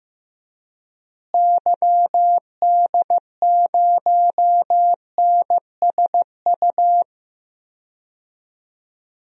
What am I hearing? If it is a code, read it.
Morse code, heard as a keyed tone, about 15 wpm, 700 Hz. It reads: YD0NSU
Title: DXCC-100 Call Signs - 6 Characters - Repeated 3x - 15wpm